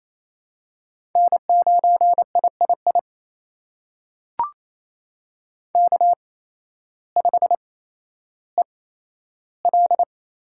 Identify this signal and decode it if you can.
Morse code, heard as a keyed tone, about 28 wpm, spaced as for 13 wpm, 700 Hz. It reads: N9III K5EL
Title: USA Call Signs - 5 Characters - 8x Character Spacing - 28wpm